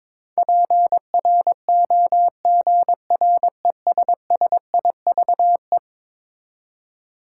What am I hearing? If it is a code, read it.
Morse code, heard as a keyed tone, about 22 wpm, 700 Hz. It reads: PROGRESSIVE
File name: practice / 22wpm / Eleven Letter Words - 22wpm